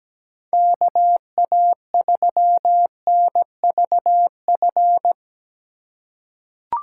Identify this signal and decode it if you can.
Morse code, heard as a keyed tone, about 17 wpm, 700 Hz. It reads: KA3NVF
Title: USA Call Signs - 6 Characters - 4x Character Spacing - 17wpm